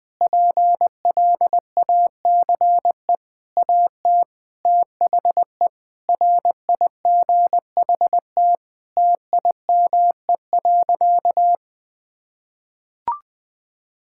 Morse code, heard as a keyed tone, about 20 wpm, 700 Hz. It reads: PLACE AT THE RIGHT TIME.
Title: Sentences with 9-Word Phrases - 20wpm